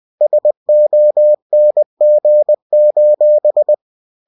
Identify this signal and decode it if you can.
Morse code, heard as a keyed tone, about 20 wpm, 600 Hz. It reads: SONG: